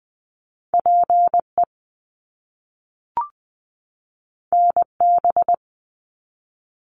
Morse code, heard as a keyed tone, about 20 wpm, 700 Hz. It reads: PE NB
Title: Canadian Provinces and Territories 20wpm